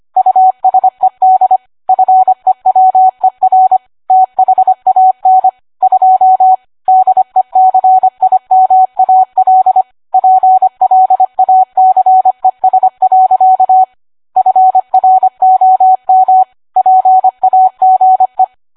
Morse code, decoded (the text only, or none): USED FEWER THAN 2 DECIMAL PLACES. FROM PAGE